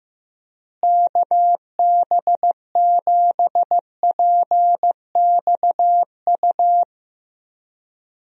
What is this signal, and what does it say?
Morse code, heard as a keyed tone, about 15 wpm, 700 Hz. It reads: KB7PXU
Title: USA Call Signs - 6 Characters - Repeated 3x - 15wpm